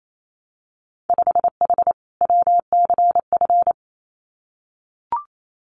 Morse code, heard as a keyed tone, about 28 wpm, 700 Hz. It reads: 5H WCF
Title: Winter Field Day Exchanges Encore 28wpm